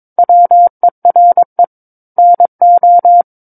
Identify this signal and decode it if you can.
Morse code, heard as a keyed tone, about 22 wpm, 700 Hz. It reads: WERE NO
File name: Book One / 13-Chapter 13 - 22 WPM